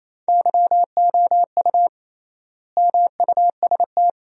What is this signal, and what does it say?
Morse code, heard as a keyed tone, about 29 wpm, 700 Hz. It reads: YOU MUST
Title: 3x Word-Spacing - Sentences from Top 300 Words - 28wpm